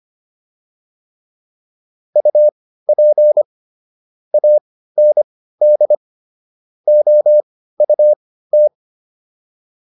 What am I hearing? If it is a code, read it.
Morse code, heard as a keyed tone, about 25 wpm, spaced as for 15 wpm, 600 Hz. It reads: UP AND OUT